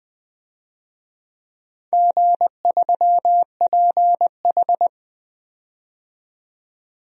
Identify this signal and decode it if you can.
Morse code, heard as a keyed tone, about 20 wpm, 700 Hz. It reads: G3PH
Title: DXCC-100 Call Signs - 4 Characters - 20wpm